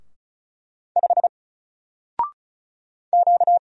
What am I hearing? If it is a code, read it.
Morse code, heard as a keyed tone, about 35 wpm, 700 Hz. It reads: 5 Q
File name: Single-Letter-Number-35wpm